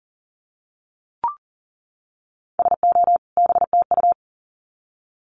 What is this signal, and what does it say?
Morse code, heard as a keyed tone, about 40 wpm, 700 Hz. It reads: SO BTU
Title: Common QSO Phrases 40wpm